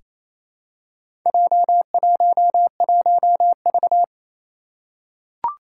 Morse code, heard as a keyed tone, about 29 wpm, 700 Hz. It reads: J11V
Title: Four Challenging Letter-Number Combinations 28wpm